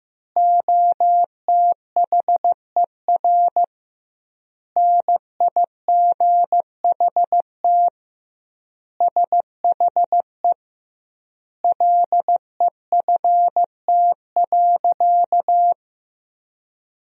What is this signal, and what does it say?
Morse code, heard as a keyed tone, about 15 wpm, 700 Hz. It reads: OTHER NIGHT SHE LEFT.
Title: 2x Word-Spacing - Sentences from Top 300 Words - 15wpm